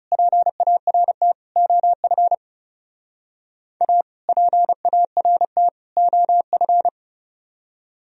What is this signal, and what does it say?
Morse code, heard as a keyed tone, about 34 wpm, 700 Hz. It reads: PART OF A PART OF
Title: Speed-Racing - Sets of 3 Words 30wpm